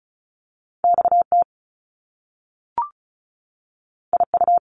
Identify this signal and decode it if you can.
Morse code, heard as a keyed tone, about 36 wpm, 700 Hz. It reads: XT IU
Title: Top Two Letter Combinations in Words 35wpm